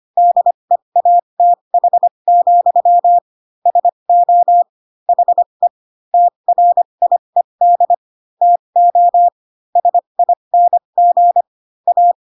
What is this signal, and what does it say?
Morse code, heard as a keyed tone, about 25 wpm, spaced as for 22 wpm, 700 Hz. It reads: DEATH, SO HE TRIED TO SING A